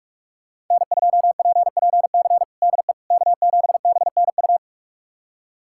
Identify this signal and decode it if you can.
Morse code, heard as a keyed tone, about 46 wpm, 700 Hz. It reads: N2PPC DE K7BNU
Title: USA Callsign DE Callsign - 5 Characters - 45wpm